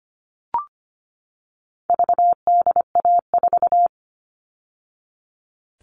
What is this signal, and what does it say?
Morse code, heard as a keyed tone, about 25 wpm, 700 Hz. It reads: VDA4
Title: All Cut Numbers - 4-Digit 25wpm